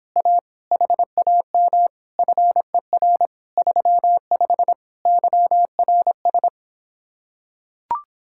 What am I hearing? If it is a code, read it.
Morse code, heard as a keyed tone, about 26 wpm, 700 Hz. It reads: A HAM FER 35 YRS